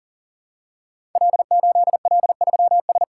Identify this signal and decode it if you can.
Morse code, heard as a keyed tone, about 42 wpm, 700 Hz. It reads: L8L3S
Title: Code Groups - 5 Letter:Numbers - 40wpm